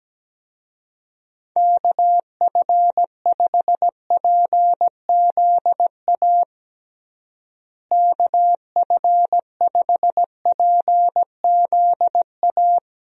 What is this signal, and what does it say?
Morse code, heard as a keyed tone, about 17 wpm, 700 Hz. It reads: KF5PZA KF5PZA
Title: USA Call Signs - 6 Characters - Repeated 3x - 17wpm